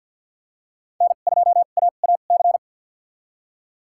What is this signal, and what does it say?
Morse code, heard as a keyed tone, about 53 wpm, spaced as for 37 wpm, 700 Hz. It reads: N2AAC
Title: USA Call Signs - 5 Characters - 2x Character Spacing - 50wpm